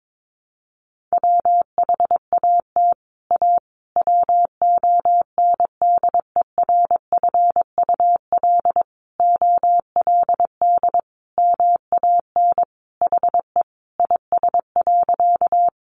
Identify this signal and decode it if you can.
Morse code, heard as a keyed tone, about 22 wpm, 700 Hz. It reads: WHAT A WONDERFUL OLD MAN HE IS.
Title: Sentences from Top 1100 Words 22wpm